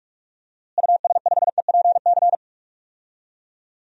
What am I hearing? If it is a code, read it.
Morse code, heard as a keyed tone, about 46 wpm, 700 Hz. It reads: US5EPC
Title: DXCC-100 Call Signs - 6 Characters - Repeated 3x - 45wpm